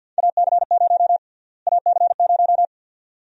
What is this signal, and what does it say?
Morse code, heard as a keyed tone, about 46 wpm, 700 Hz. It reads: AC0 AC0
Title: USA Call Sign Prefixes - Repeated 2x - 50wpm